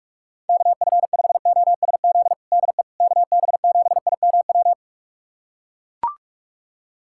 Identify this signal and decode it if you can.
Morse code, heard as a keyed tone, about 42 wpm, 700 Hz. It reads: KF5OSZ DE KB7IMW
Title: USA Callsign DE Callsign 45wpm